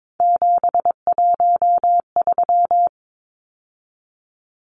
Morse code, heard as a keyed tone, about 22 wpm, 700 Hz. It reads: 713
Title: Numbers - 3-Digit 22wpm